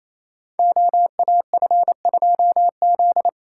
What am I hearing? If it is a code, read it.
Morse code, heard as a keyed tone, about 28 wpm, 700 Hz. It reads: OAF2Z
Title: Code Groups - 5 Letter:Numbers - 28wpm